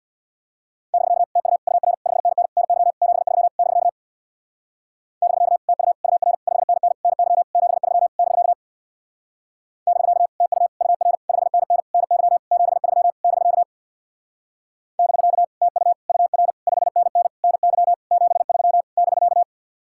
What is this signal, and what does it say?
Morse code, heard as a keyed tone, about 64 wpm, 700 Hz. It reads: <BK> TU UR 5NN NY 73 <BK> <BK> TU UR 5NN NY 73 <BK> <BK> TU UR 5NN NY 73 <BK> <BK> TU UR 5NN NY 73 <BK>